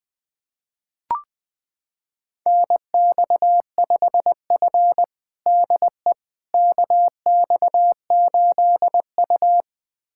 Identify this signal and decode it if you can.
Morse code, heard as a keyed tone, about 20 wpm, 700 Hz. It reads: NX5F DE KX8U
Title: USA Callsign DE Callsign - 4 Characters - 20wpm